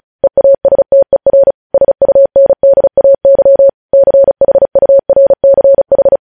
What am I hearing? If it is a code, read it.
Morse code, heard as a keyed tone, about 36 wpm, 550 Hz. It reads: EASTER SUNDAY CHURCH
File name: podcast / News headlines-1x-35x35